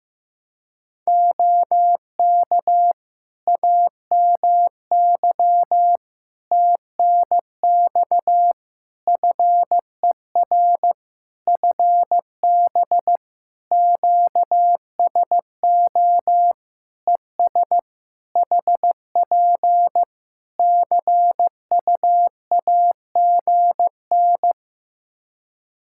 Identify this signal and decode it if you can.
Morse code, heard as a keyed tone, about 16 wpm, 700 Hz. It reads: OK AMY TNX FER FB QSO ES HP CUAGN
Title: Practice QSO - Exchange 4 - Ending - 15wpm